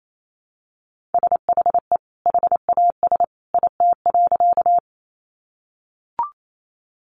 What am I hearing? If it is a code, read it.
Morse code, heard as a keyed tone, about 28 wpm, 700 Hz. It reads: SHE HAS IT.
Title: Sentences from Top 100 Words 28wpm